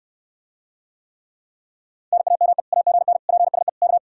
Morse code, heard as a keyed tone, about 69 wpm, 700 Hz. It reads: NAME AGN PSE K